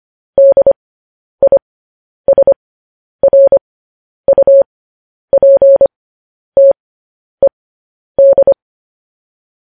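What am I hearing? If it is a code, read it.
Morse code, heard as a keyed tone, about 25 wpm, spaced as for 10 wpm, 550 Hz. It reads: DISRUPTED
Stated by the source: Morse code news headlines broadcast at 10 wpm.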